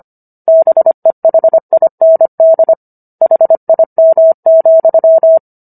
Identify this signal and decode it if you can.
Morse code, heard as a keyed tone, about 25 wpm, 650 Hz. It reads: BEHIND HIM,